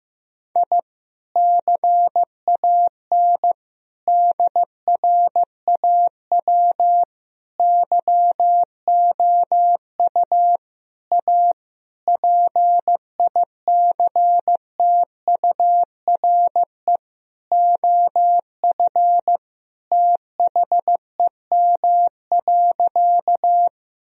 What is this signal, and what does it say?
Morse code, heard as a keyed tone, about 15 wpm, 700 Hz. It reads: I CAN DRAW YOU A PICTURE OF THEM.
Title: Sentences from Top 1000 Words - Review Part 1 - 15wpm